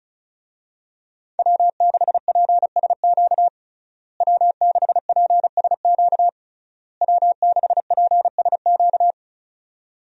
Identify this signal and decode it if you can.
Morse code, heard as a keyed tone, about 36 wpm, 700 Hz. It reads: W6PSQ W6PSQ W6PSQ
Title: USA Call Signs - 5 Characters - Repeated 3x - 35wpm